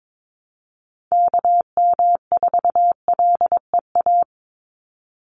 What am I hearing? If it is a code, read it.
Morse code, heard as a keyed tone, about 22 wpm, 700 Hz. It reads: KM4LEA
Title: USA Call Signs 22wpm